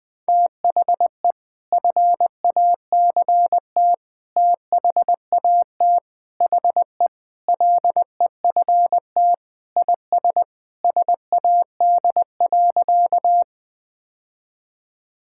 Morse code, heard as a keyed tone, about 20 wpm, 700 Hz. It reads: THE FACT THAT HE LEFT IS SAD.
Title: Sentences with 2-Word Phrases - 20wpm